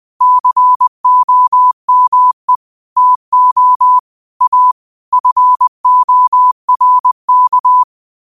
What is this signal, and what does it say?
Morse code, heard as a keyed tone, about 20 wpm, 1000 Hz. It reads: COME TO A FORK
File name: 20wpm